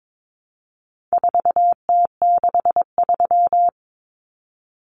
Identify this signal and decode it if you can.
Morse code, heard as a keyed tone, about 22 wpm, 700 Hz. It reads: 4T63
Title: Common Cut Numbers - 4-Digit 22wpm